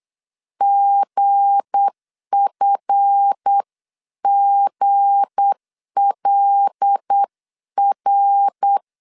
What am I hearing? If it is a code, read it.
Morse code, heard as a keyed tone, about 8 wpm, 800 Hz. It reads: GFGLR
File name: CW / DET / 1 9 1500C S B 8wpm 5624